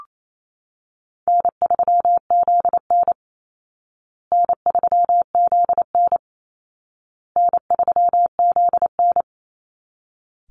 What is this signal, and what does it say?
Morse code, heard as a keyed tone, about 28 wpm, 700 Hz. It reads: N3ZN N3ZN N3ZN
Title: Easy Call Signs Repeated 3x - 28wpm